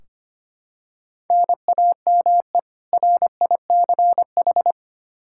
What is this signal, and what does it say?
Morse code, heard as a keyed tone, about 25 wpm, 700 Hz. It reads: NAME RICH
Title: K1USN SST QSO Practice 25wpm